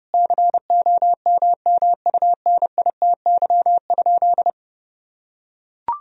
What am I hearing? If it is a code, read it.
Morse code, heard as a keyed tone, about 32 wpm, 700 Hz. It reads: COMMUNITY?
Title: Sentences from Top 1000 Words - Review Part 4 - 30wpm